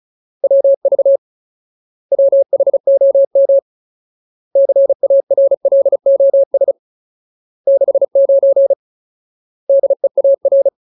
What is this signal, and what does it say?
Morse code, heard as a keyed tone, about 34 wpm, 550 Hz. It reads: WV WHOM CARLOS 69 DEAR
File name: Ultimate Word List / 35WPM Ultimate